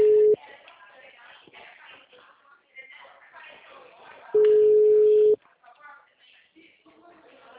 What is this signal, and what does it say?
Ringback tone (Europe, 425 Hz), with speech.